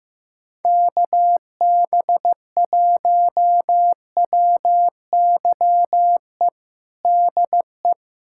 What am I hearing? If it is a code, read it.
Morse code, heard as a keyed tone, about 15 wpm, 700 Hz. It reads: KB1WYE DE
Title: USA Callsign DE Callsign - 6 Characters - 15wpm